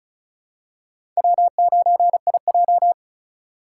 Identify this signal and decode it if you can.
Morse code, heard as a keyed tone, about 36 wpm, 700 Hz. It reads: W9IJ
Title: USA Call Signs - 4 Characters - 3x Character Spacing - 35wpm